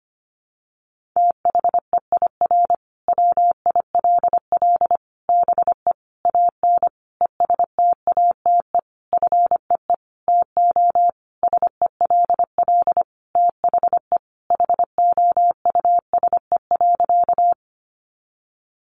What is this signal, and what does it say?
Morse code, heard as a keyed tone, about 25 wpm, 700 Hz. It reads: THEIR WILL BE AN ESTATE FEE TO SELL THE HOUSE.
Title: Sentences from Top 1500 Words 25wpm